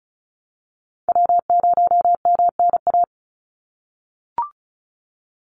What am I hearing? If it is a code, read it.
Morse code, heard as a keyed tone, about 34 wpm, 700 Hz. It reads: W0MNA
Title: NA SOTA Chaser Callsigns 35wpm